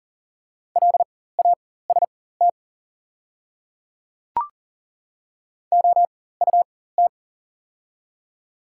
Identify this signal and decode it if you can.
Morse code, heard as a keyed tone, about 40 wpm, spaced as for 19 wpm, 690 Hz. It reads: LAST OUT
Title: 4x Character Spacing - Top 500 Words - 40wpm